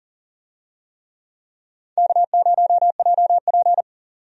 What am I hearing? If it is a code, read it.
Morse code, heard as a keyed tone, about 40 wpm, 700 Hz. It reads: K0JP
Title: USA Call Signs - 4 Characters - 40wpm